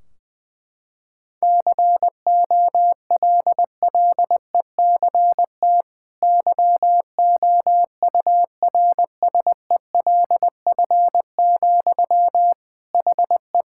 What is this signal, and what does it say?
Morse code, heard as a keyed tone, about 20 wpm, 700 Hz. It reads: COLLECT YOURSELF, HE
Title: Sentences from Top 1600 Words 20wpm